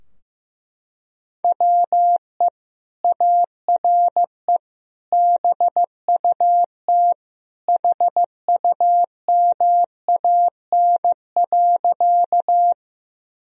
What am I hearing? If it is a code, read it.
Morse code, heard as a keyed tone, about 15 wpm, 700 Hz. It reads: WE ARE BUT HUMAN.
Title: Sentences from Top 1000 Words - Review Part 3 - 15wpm